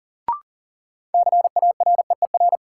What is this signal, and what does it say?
Morse code, heard as a keyed tone, about 40 wpm, 700 Hz. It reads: CAREER
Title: Sentences from Top 1000 Words - New Word Drill - 40wpm